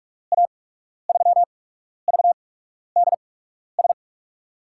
Morse code, heard as a keyed tone, about 44 wpm, spaced as for 12 wpm, 700 Hz. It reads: A3VDS
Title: USA Call Signs - 6 Characters - 8x Character Spacing - 45wpm